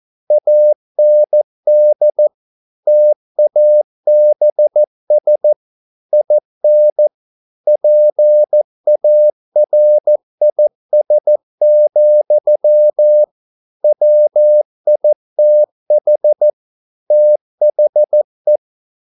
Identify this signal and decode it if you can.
Morse code, heard as a keyed tone, about 14 wpm, 600 Hz. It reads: AND TABS IN PARIS, WITH THE